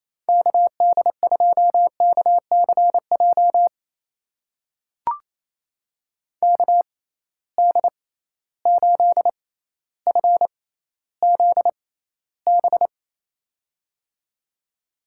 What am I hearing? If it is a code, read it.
Morse code, heard as a keyed tone, about 28 wpm, spaced as for 14 wpm, 700 Hz. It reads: KD2KCJ KD8FZB
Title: USA Call Signs - 6 Characters - 6x Character Spacing - 28wpm